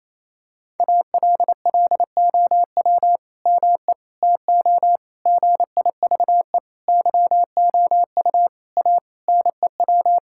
Morse code, heard as a keyed tone, about 28 wpm, 700 Hz. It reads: ALLOW ME TO GIVE YOU A NEW
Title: Sentences from Top 1000 Words - Review Part 5 - 28wpm